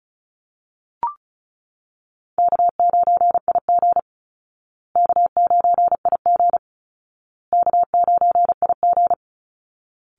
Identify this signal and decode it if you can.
Morse code, heard as a keyed tone, about 34 wpm, 700 Hz. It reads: K9IG K9IG K9IG
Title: Call Signs Repeated 2x - 35wpm